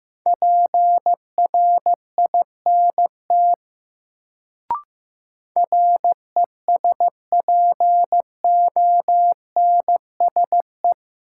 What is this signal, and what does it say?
Morse code, heard as a keyed tone, about 15 wpm, 700 Hz. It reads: PRINT RESPONSE
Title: Sentences from Top 1400 Words - New Word Drill - 15wpm